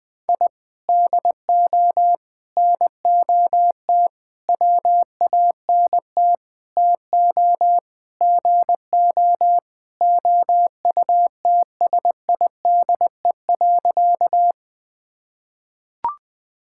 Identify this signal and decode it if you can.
Morse code, heard as a keyed tone, about 21 wpm, 700 Hz. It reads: I DO NOT WANT TO GO OUTSIDE.
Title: Sentences with 5-Word Phrases - 20wpm